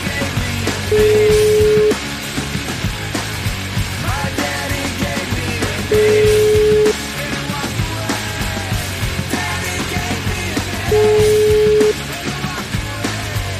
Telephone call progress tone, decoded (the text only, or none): ringback tone